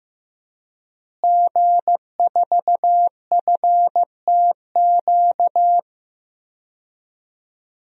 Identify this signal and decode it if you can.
Morse code, heard as a keyed tone, about 16 wpm, 700 Hz. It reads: G4FTQ
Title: DXCC-100 Call Signs - 5 Characters - Repeated 2x - 15wpm